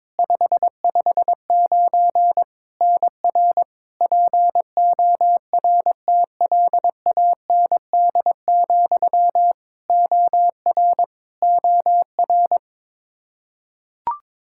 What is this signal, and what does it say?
Morse code, heard as a keyed tone, about 22 wpm, 700 Hz. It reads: 559 NR PORTLAND, OR OR
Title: Practice QSO - Exchange 1 - Beginning - 22wpm